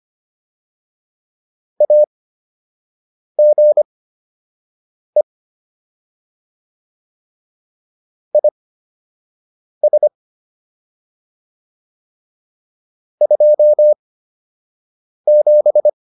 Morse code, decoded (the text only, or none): AGE IS 27